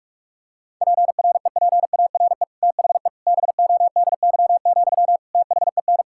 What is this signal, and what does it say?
Morse code, heard as a keyed tone, about 46 wpm, 700 Hz. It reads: PREPARE THE BODY, THEN
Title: Sentences from Top 1000 Words - Review Part 6 - 45wpm